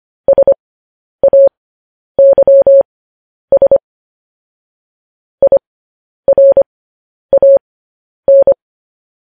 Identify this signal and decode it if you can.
Morse code, heard as a keyed tone, about 25 wpm, spaced as for 10 wpm, 550 Hz. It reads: SAYS IRAN